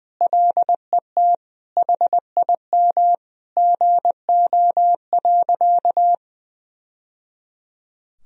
Morse code, heard as a keyed tone, about 20 wpm, 700 Hz. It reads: LET HIM GO.
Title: Sentences from Top 1000 Words - Review Part 1 - 20wpm